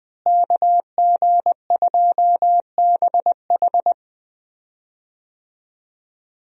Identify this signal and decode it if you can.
Morse code, heard as a keyed tone, about 20 wpm, 700 Hz. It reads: KG2BH
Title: USA Call Signs - 5 Characters - Repeated 3x - 20wpm